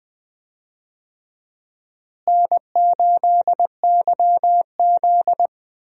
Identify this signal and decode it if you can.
Morse code, heard as a keyed tone, about 20 wpm, 700 Hz. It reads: N8YZ
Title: USA Call Signs - 4 Characters - Repeated 3x - 20wpm